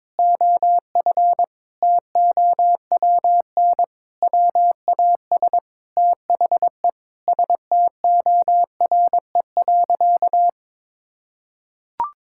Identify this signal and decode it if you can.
Morse code, heard as a keyed tone, about 22 wpm, 700 Hz. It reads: OF TOWN WAS THE STORE.
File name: practice / 22wpm / Sentences with 4-Word Phrases - 22wpm